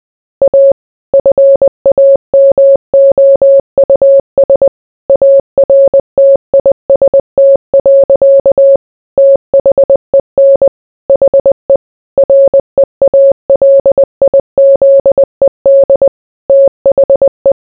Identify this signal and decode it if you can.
Morse code, heard as a keyed tone, about 20 wpm, 550 Hz. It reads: A FAMOUS ARTIST. THEN HE REALIZED THE